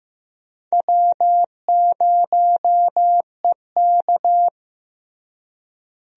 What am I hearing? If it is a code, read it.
Morse code, heard as a keyed tone, about 15 wpm, 700 Hz. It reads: W0EK
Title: USA Call Signs - 4 Characters - Repeated 3x - 15wpm